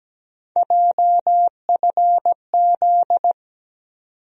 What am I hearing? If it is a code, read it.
Morse code, heard as a keyed tone, about 17 wpm, 700 Hz. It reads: JFZ
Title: USA Call Sign Postfixes - 3 Letters 17wpm